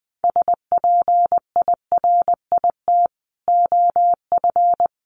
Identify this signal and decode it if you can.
Morse code, heard as a keyed tone, about 20 wpm, 700 Hz. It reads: SPIRIT OF
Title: Sentences from Top 1400 Words 20wpm